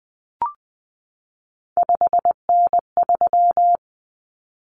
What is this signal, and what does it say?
Morse code, heard as a keyed tone, about 20 wpm, 700 Hz. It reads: 5N3
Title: Common Cut Numbers - 3-Digit 20wpm